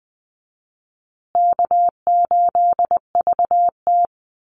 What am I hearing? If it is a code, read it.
Morse code, heard as a keyed tone, about 20 wpm, 700 Hz. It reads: K8VT
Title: USA Call Signs Repeated 2x 20wpm